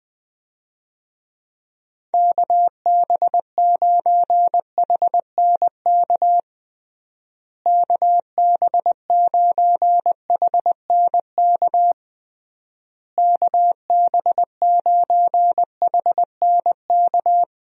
Morse code, heard as a keyed tone, about 20 wpm, 700 Hz. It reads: KB9HNK KB9HNK KB9HNK
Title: USA Call Signs - 6 Characters - Repeated 3x - 20wpm